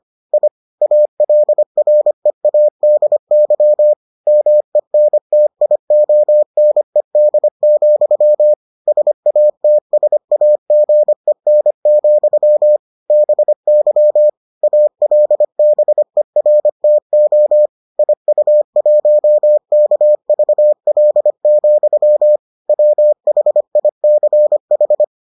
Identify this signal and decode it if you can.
Morse code, heard as a keyed tone, about 25 wpm, 600 Hz. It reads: I ALREADY MENTIONED, SATSAGEN, BY ALBERTO IU1KVL, WHICH